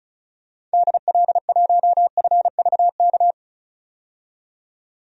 Morse code, heard as a keyed tone, about 34 wpm, 700 Hz. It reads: DL1FVK
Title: DXCC-100 Call Signs - 6 Characters - 35wpm